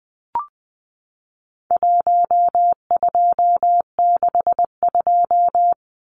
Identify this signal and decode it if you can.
Morse code, heard as a keyed tone, about 20 wpm, 700 Hz. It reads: 1262
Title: Numbers - 4-Digit 20wpm